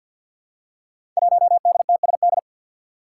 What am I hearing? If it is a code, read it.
Morse code, heard as a keyed tone, about 46 wpm, 700 Hz. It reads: 1DTSD
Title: Five Challenging Letter-Number Combinations 50wpm